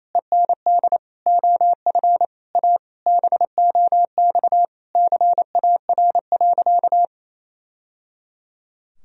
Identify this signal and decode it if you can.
Morse code, heard as a keyed tone, about 28 wpm, 700 Hz. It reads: END OF A BOX CAR.
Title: Sentences from Top 1000 Words - Review Part 2 - 28wpm